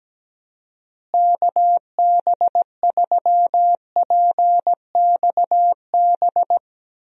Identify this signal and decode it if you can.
Morse code, heard as a keyed tone, about 17 wpm, 700 Hz. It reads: KB3PXB
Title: USA Call Signs - 6 Characters - Repeated 2x - 17wpm